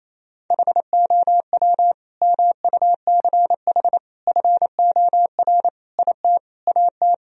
Morse code, heard as a keyed tone, about 28 wpm, 700 Hz. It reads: HOW MUCH FOR IT AT
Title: Sentences from Top 1000 Words - Review Part 3 - 28wpm